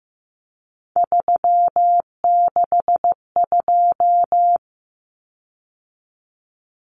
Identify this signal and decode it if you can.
Morse code, heard as a keyed tone, about 15 wpm, 700 Hz. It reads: 362
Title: Common Cut Numbers - 3-Digit 15wpm